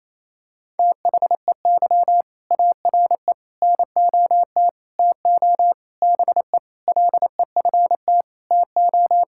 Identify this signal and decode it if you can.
Morse code, heard as a keyed tone, about 28 wpm, 700 Hz. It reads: THEY ARE NOT TO BE LEFT TO
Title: Sentences from Top 1000 Words - Review Part 5 - 28wpm